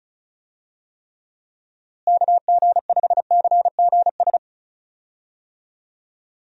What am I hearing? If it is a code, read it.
Morse code, heard as a keyed tone, about 36 wpm, 700 Hz. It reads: KG5CGS
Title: USA Call Signs - 6 Characters - Repeated 3x - 35wpm